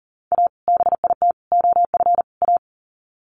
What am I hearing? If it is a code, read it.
Morse code, heard as a keyed tone, about 40 wpm, 700 Hz. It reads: A BIT OF A
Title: Sets of 4 Words 40wpm